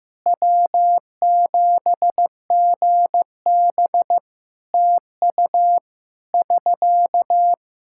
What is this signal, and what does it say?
Morse code, heard as a keyed tone, about 15 wpm, 700 Hz. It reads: W7GB TU <SK>
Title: Practice QSO - Exchange 4 - Ending - 15wpm